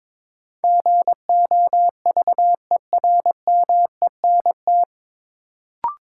Morse code, heard as a keyed tone, about 22 wpm, 700 Hz. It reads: GOVERMENT
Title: Sentences from Top 1300 Words - New Word Drill Alt - 22wpm